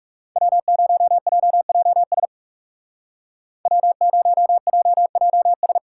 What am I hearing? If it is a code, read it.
Morse code, heard as a keyed tone, about 36 wpm, 700 Hz. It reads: W0JJS W0JJS
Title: Speed Racing - USA 1x3 Call Signs 30wpm